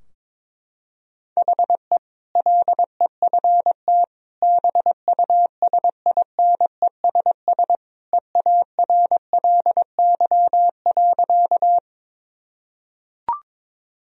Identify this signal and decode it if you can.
Morse code, heard as a keyed tone, about 22 wpm, 700 Hz. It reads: HE LEFT BUSINESS EARLY.
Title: Sentences from Top 1000 Words - Review Part 2 - 22wpm